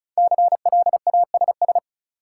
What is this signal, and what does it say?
Morse code, heard as a keyed tone, about 34 wpm, 700 Hz. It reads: CLASS
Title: Sentences from Top 1400 Words - New Word Drill Alt - 35wpm